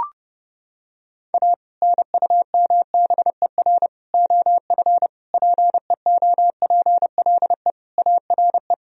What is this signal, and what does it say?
Morse code, heard as a keyed tone, about 30 wpm, 700 Hz. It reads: A NUMBER OF PEOPLE ARE
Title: Sentences from Top 1000 Words - Review Part 1 - 30wpm